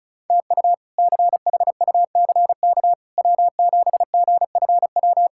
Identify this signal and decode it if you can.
Morse code, heard as a keyed tone, about 34 wpm, 700 Hz. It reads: TU CHUCK W7GFW
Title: K1USN SST QSO Practice - 35wpm